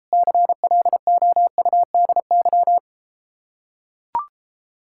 Morse code, heard as a keyed tone, about 33 wpm, 700 Hz. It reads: CLOUDY